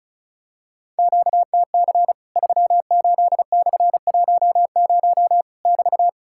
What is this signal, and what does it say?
Morse code, heard as a keyed tone, about 36 wpm, 700 Hz. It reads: QTC 38/10 <BT>
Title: Worked All Europe DX Contest - QTC Exchange - 35wpm